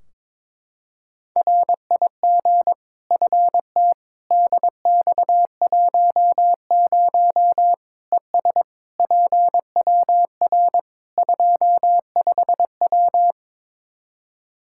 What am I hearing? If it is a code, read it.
Morse code, heard as a keyed tone, about 21 wpm, 700 Hz. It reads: RIG FT DX10 ES PWR 25W
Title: Practice QSO - Exchange 2 - Info - 22wpm